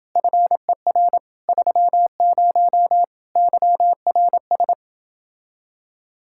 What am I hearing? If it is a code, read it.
Morse code, heard as a keyed tone, about 29 wpm, 700 Hz. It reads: FER 30 YRS